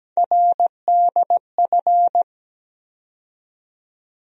Morse code, heard as a keyed tone, about 17 wpm, 700 Hz. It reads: RDF
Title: USA Call Sign Postfixes - 3 Letters 17wpm